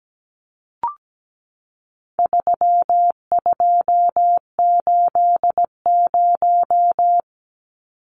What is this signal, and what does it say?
Morse code, heard as a keyed tone, about 17 wpm, 700 Hz. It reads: 3280
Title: Numbers - 4-Digit 17wpm